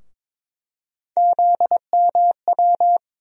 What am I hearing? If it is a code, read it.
Morse code, heard as a keyed tone, about 22 wpm, 700 Hz. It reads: ZMW
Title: USA Call Sign Postfixes - 3 Letters 22wpm